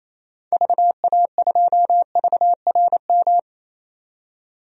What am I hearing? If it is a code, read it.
Morse code, heard as a keyed tone, about 28 wpm, 700 Hz. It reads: VA2VRM
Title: DXCC-100 Call Signs - 6 Characters - 28wpm